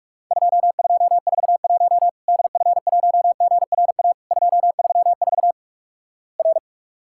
Morse code, heard as a keyed tone, about 44 wpm, 700 Hz. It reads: 2241 DF1GRA 234 R
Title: Worked All Europe DX Contest - QTC Exchange - 45wpm